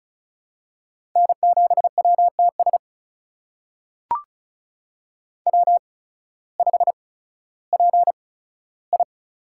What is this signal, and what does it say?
Morse code, heard as a keyed tone, about 36 wpm, spaced as for 16 wpm, 700 Hz. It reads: N7WTS W5PI
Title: USA Call Signs - 8x Character Spacing - 35wpm